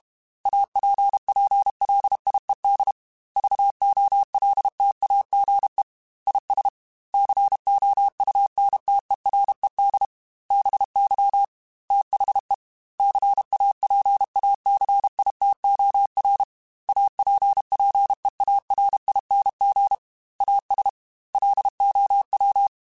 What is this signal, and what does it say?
Morse code, heard as a keyed tone, about 32 wpm, 800 Hz. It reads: APPLIED VOLTAGE IS COUNTERED BY THE CAPACITOR APPEARING AS LOW